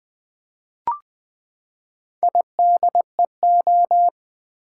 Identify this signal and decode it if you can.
Morse code, heard as a keyed tone, about 20 wpm, 700 Hz. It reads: IDEO
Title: Build A Word - 4 Letters At A Time - 11-Letter Words - 20wpm